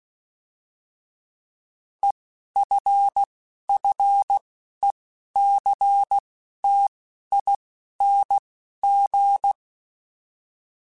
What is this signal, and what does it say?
Morse code, heard as a keyed tone, about 16 wpm, spaced as for 11 wpm, 800 Hz. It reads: EFFECTING
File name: Alphabetical / Alphabetical mp3s / Es @10